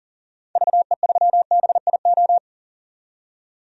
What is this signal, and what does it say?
Morse code, heard as a keyed tone, about 39 wpm, 700 Hz. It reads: VE3BIO
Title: DXCC-100 Call Signs - 6 Characters - 40wpm